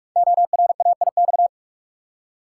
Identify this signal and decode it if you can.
Morse code, heard as a keyed tone, about 44 wpm, 700 Hz. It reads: ORAIX
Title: Code Groups - 5 Letters - 45wpm